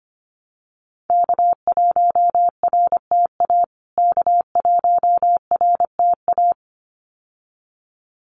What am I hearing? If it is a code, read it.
Morse code, heard as a keyed tone, about 25 wpm, 700 Hz. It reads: K1RTA K1RTA
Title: USA Call Signs Repeated 2x 25wpm